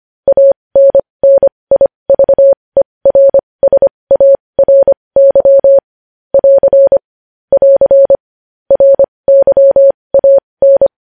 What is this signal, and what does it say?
Morse code, heard as a keyed tone, about 25 wpm, spaced as for 20 wpm, 550 Hz. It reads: ANNIVERSARY <AR> <AR> RYAN
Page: news-headlines-in-morse-code-at-20-wpm